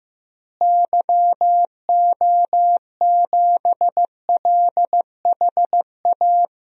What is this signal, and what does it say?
Morse code, heard as a keyed tone, about 15 wpm, 700 Hz. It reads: YO7LHA
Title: DXCC-100 Call Signs - 6 Characters - Repeated 3x - 15wpm